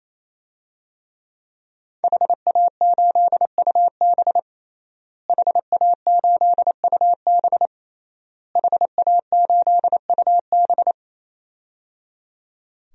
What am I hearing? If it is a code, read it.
Morse code, heard as a keyed tone, about 28 wpm, 700 Hz. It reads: HA8UB HA8UB HA8UB
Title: DXCC-100 Call Signs - 5 Characters - Repeated 3x - 28wpm